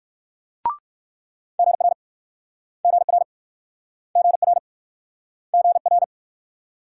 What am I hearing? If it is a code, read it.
Morse code, heard as a keyed tone, about 61 wpm, 700 Hz. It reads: GR GR GR GR